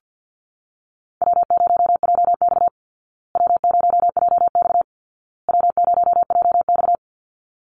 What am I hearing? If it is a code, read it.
Morse code, heard as a keyed tone, about 49 wpm, 700 Hz. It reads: W0JX W0JX W0JX
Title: Easy Call Signs Repeated 3x - 50wpm